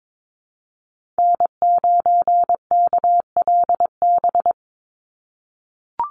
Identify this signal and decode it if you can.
Morse code, heard as a keyed tone, about 22 wpm, 700 Hz. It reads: N9KLB
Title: USA Call Signs 22wpm